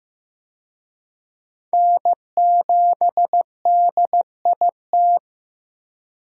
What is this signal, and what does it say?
Morse code, heard as a keyed tone, about 16 wpm, 700 Hz. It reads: N7DIT
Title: USA Call Signs - 5 Characters - 15wpm